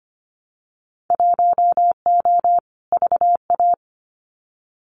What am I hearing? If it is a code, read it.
Morse code, heard as a keyed tone, about 25 wpm, 700 Hz. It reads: 1O VA
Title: Winter Field Day Exchanges 25wpm